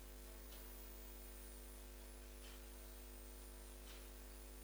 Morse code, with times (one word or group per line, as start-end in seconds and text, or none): none